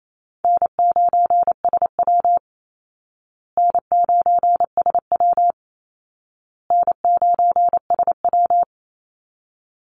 Morse code, heard as a keyed tone, about 28 wpm, 700 Hz. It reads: N9SW N9SW N9SW
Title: Easy Call Signs Repeated 3x - 28wpm